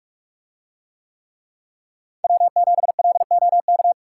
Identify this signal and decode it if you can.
Morse code, heard as a keyed tone, about 44 wpm, 700 Hz. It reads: W7LOK